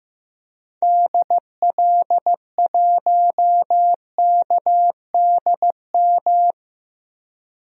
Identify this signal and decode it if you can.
Morse code, heard as a keyed tone, about 15 wpm, 700 Hz. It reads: DL1KDM
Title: DXCC-100 Call Signs - 6 Characters - 15wpm